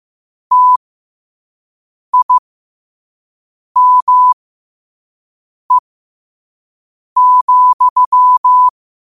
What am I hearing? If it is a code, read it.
Morse code, heard as a keyed tone, about 15 wpm, spaced as for 5 wpm, 1000 Hz. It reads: TIME,
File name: qotdfiles / 5wpm